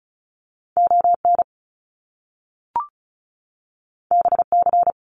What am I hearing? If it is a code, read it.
Morse code, heard as a keyed tone, about 36 wpm, 700 Hz. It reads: ON BC
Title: Canadian Provinces and Territories 35wpm